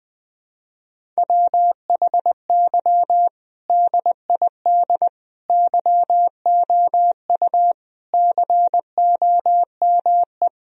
Morse code, heard as a keyed tone, about 20 wpm, 700 Hz. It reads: WHY DID YOU COME
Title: Sentences from Top 1000 Words - Review Part 1 - 20wpm